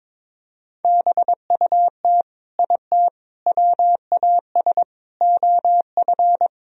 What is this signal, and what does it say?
Morse code, heard as a keyed tone, about 22 wpm, 700 Hz. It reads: BUT IT WAS OF
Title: Sentences from Top 100 Words and Abbreviations 22wpm